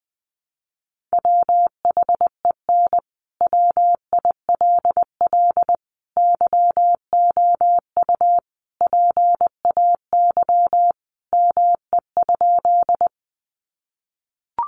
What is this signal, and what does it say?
Morse code, heard as a keyed tone, about 20 wpm, 700 Hz. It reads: WHEN WILL YOU PAY ME?
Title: Sentences from Top 700 Words 20wpm